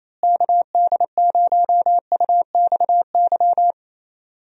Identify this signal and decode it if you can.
Morse code, heard as a keyed tone, about 28 wpm, 700 Hz. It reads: KD0UXY
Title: USA Call Signs - 6 Characters - Repeated 3x - 28wpm